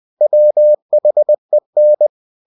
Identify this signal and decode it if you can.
Morse code, heard as a keyed tone, about 20 wpm, 600 Hz. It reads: WHEN
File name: cw-books / winnie-the-pooh / winnie-20wpm-page-0070